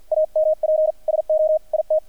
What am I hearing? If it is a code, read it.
Morse code, heard as a keyed tone, about 50 wpm, 620 Hz, with radio noise. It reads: AMP SO IT
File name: archived / CRSnet